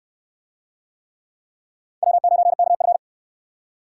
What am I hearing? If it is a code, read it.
Morse code, heard as a keyed tone, about 69 wpm, 700 Hz. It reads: W9KF